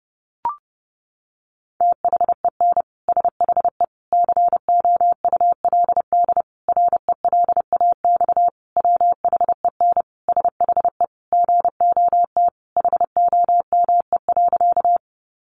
Morse code, heard as a keyed tone, about 30 wpm, 700 Hz. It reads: THEN SHE COULD RELAX WHEN SHE GOT HOME.
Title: Sentences from Top 1400 Words 30wpm